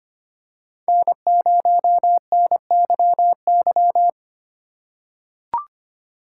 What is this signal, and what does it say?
Morse code, heard as a keyed tone, about 25 wpm, 700 Hz. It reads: N0NYY
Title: USA Call Signs - 5 Characters - 3x Character Spacing - 25wpm